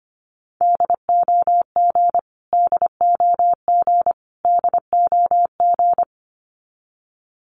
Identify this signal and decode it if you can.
Morse code, heard as a keyed tone, about 25 wpm, 700 Hz. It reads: DOG DOG DOG
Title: Top 500 Words Repeated 25wpm